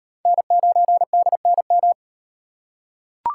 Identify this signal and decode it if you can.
Morse code, heard as a keyed tone, about 33 wpm, 700 Hz. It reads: N9DNM